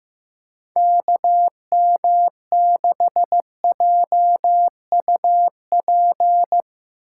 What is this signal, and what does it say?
Morse code, heard as a keyed tone, about 15 wpm, 700 Hz. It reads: KM6JUP
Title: USA Call Signs - 6 Characters - 4x Character Spacing - 15wpm